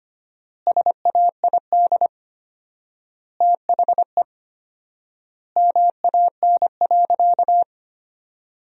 Morse code, heard as a keyed tone, about 25 wpm, 700 Hz. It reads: SAID THE MAN.
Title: 4x Word-Spacing - Sentences from Top 400 Words - 25wpm